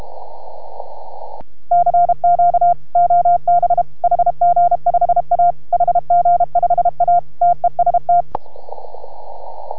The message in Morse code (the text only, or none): CQ OB HG5A HG5A TEST